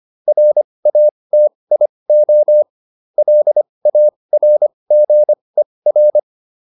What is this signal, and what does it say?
Morse code, heard as a keyed tone, about 25 wpm, spaced as for 20 wpm, 600 Hz. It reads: RATIO LARGER
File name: SS316 20